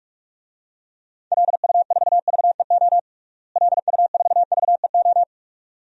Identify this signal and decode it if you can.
Morse code, heard as a keyed tone, about 46 wpm, 700 Hz. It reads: LU4VEO LU4VEO
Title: DXCC-100 Call Signs - 6 Characters - Repeated 2x - 45wpm